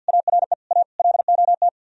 Morse code, heard as a keyed tone, about 64 wpm, 700 Hz. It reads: ARE A LOT